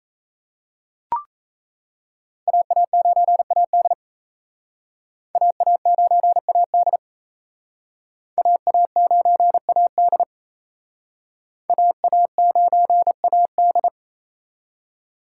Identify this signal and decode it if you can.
Morse code, heard as a keyed tone, about 34 wpm, 700 Hz. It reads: AA9AD AA9AD AA9AD AA9AD
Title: Speed Racing - USA 2x2 Call Signs 28wpm